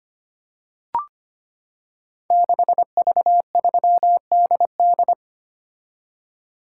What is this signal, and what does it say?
Morse code, heard as a keyed tone, about 25 wpm, 700 Hz. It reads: 6V3DD
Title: Code Groups - 5 Letter:Numbers - 25wpm